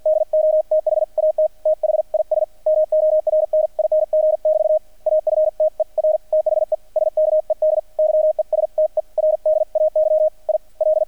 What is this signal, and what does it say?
Morse code, heard as a keyed tone, about 50 wpm, spaced as for 44 wpm, 620 Hz, with radio noise. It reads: DO THAT THIS MORNING <BT> AFTER THE SKED YESTERDAY I L
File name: archived / CRSnet